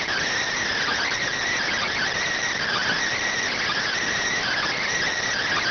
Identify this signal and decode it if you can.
Emergency Alert System (SAME) tones, heard as harsh buzzing bursts. It is heard from 0.8 to 4.8 s.